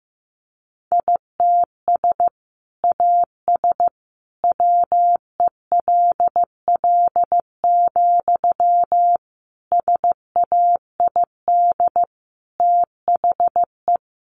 Morse code, heard as a keyed tone, about 15 wpm, 700 Hz. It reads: ITS AS WELL, SAID THE
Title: Sentences from Top 600 Words 15wpm